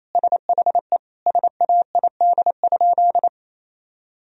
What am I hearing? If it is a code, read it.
Morse code, heard as a keyed tone, about 28 wpm, 700 Hz. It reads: SHE SAID?
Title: Sentences from Top 1000 Words - Review Part 1 - 28wpm